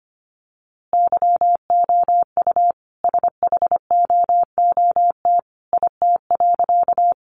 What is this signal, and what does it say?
Morse code, heard as a keyed tone, about 25 wpm, 700 Hz. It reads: YOU SHOOT IT.
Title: A Sound Of Thunder 25wpm